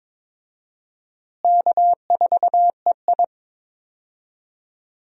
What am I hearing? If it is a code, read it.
Morse code, heard as a keyed tone, about 21 wpm, 700 Hz. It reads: K4EI
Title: USA Call Signs - 4 Characters - Repeated 3x - 22wpm